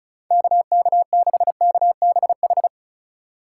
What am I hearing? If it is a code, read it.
Morse code, heard as a keyed tone, about 45 wpm, 700 Hz. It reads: KK6KBH